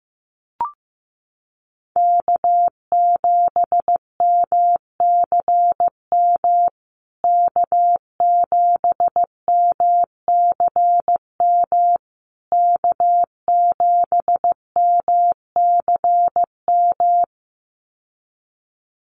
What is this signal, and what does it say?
Morse code, heard as a keyed tone, about 15 wpm, 700 Hz. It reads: K7MCM K7MCM K7MCM
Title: USA Call Signs Repeated 3x 15wpm